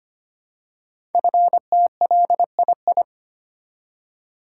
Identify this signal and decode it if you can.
Morse code, heard as a keyed tone, about 25 wpm, 700 Hz. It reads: FTLII
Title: Code Groups - 5 Letters - 25wpm